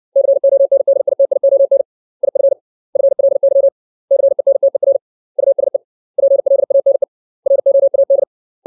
Morse code, heard as a keyed tone, about 60 wpm, 550 Hz. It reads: CONDITION IF FLY CENTER USE PLANE ROAD
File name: 632 words / 632 Words 60WPM